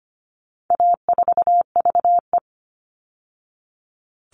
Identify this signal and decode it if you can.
Morse code, heard as a keyed tone, about 25 wpm, 700 Hz. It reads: A4VE
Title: All Cut Numbers - 4-Digit 25wpm